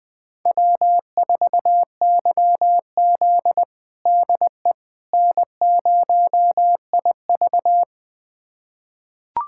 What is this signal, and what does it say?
Morse code, heard as a keyed tone, about 20 wpm, 700 Hz. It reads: W4YZ DE N0IV
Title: USA Callsign DE Callsign - 4 Characters - 20wpm